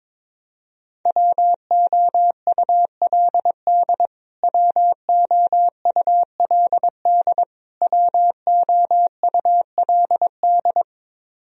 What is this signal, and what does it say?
Morse code, heard as a keyed tone, about 22 wpm, 700 Hz. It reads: WOULD WOULD WOULD
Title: Top 100 Words Repeated 22wpm